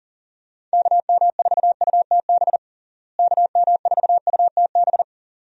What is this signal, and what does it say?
Morse code, heard as a keyed tone, about 37 wpm, 700 Hz. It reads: KM4UTB KM4UTB
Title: USA Call Signs - 6 Characters - Repeated 2x - 40wpm